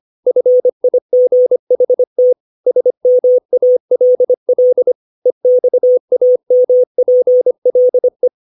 Morse code, heard as a keyed tone, about 25 wpm, 500 Hz. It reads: FIGHT SMALL EXAMPLE
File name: 632 words / 632 Words 25WPM